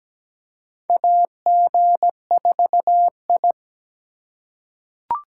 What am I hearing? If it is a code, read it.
Morse code, heard as a keyed tone, about 17 wpm, 700 Hz. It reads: AG4I
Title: USA Call Signs - 4 Characters - 3x Character Spacing - 17wpm